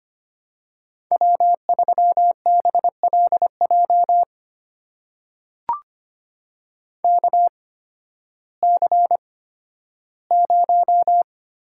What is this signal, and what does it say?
Morse code, heard as a keyed tone, about 25 wpm, spaced as for 13 wpm, 700 Hz. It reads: W3BLJ KC0
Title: USA Call Signs - 5 Characters - 8x Character Spacing - 25wpm